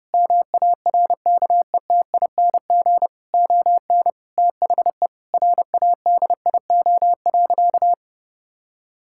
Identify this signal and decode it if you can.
Morse code, heard as a keyed tone, about 30 wpm, 700 Hz. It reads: MARKETING ON THE RADIO.
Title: Sentences from Top 1000 Words - Review Part 5 - 30wpm